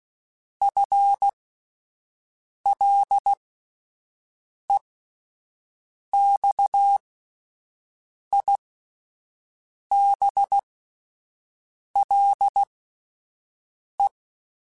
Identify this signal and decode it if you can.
Morse code, heard as a keyed tone, about 16 wpm, spaced as for 5 wpm, 800 Hz. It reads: FLEXIBLE
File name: Spaced / FL Runs sp mp3s / FL Runs sp @10